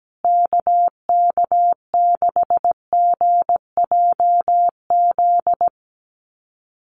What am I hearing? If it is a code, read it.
Morse code, heard as a keyed tone, about 17 wpm, 700 Hz. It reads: KK6GJZ
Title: USA Call Signs Repeated 3x 17wpm